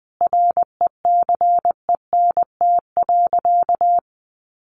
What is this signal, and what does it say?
Morse code, heard as a keyed tone, about 20 wpm, 700 Hz. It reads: RECENT.
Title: Sentences from Top 1400 Words 20wpm